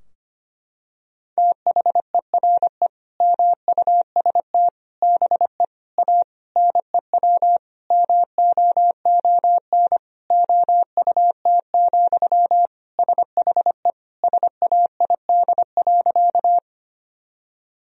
Morse code, heard as a keyed tone, about 25 wpm, 700 Hz. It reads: THERE MUST BE A NEW MOON OUT, SHE SAID.
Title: Sentences from Top 1000 Words - Review Part 3 - 25wpm